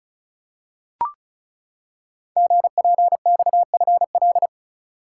Morse code, heard as a keyed tone, about 36 wpm, 700 Hz. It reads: GPXFL
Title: Code Groups - 5 Letters - 35wpm